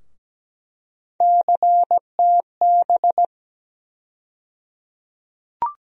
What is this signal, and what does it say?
Morse code, heard as a keyed tone, about 17 wpm, 700 Hz. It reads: CTB
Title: USA Call Sign Postfixes - 3 Letters 17wpm